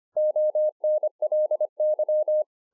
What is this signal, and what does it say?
Morse code, heard as a keyed tone, about 25 wpm, 610 Hz. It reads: ONLY